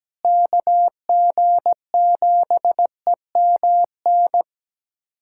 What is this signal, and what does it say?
Morse code, heard as a keyed tone, about 17 wpm, 700 Hz. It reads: KG7EMN
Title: USA Callsign DE Callsign - 6 Characters - 17wpm